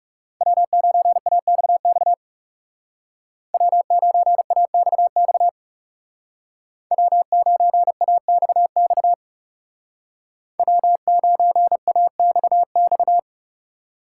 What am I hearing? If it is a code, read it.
Morse code, heard as a keyed tone, about 37 wpm, 700 Hz. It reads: W9AXX W9AXX W9AXX W9AXX
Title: Speed Racing - USA Call Signs 30wpm